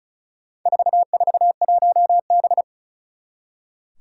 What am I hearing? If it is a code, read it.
Morse code, heard as a keyed tone, about 34 wpm, 700 Hz. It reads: 441B
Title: Four Challenging Letter-Number Combinations 35wpm